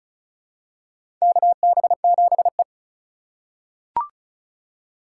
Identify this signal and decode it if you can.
Morse code, heard as a keyed tone, about 34 wpm, 700 Hz. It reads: KB7E
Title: USA Call Signs - 2x Character Spacing - 35wpm